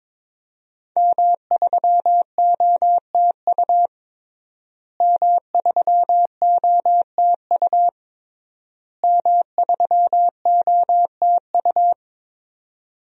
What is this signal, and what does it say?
Morse code, heard as a keyed tone, about 22 wpm, 700 Hz. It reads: M3OTU M3OTU M3OTU
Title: DXCC-100 Call Signs - 5 Characters - Repeated 3x - 22wpm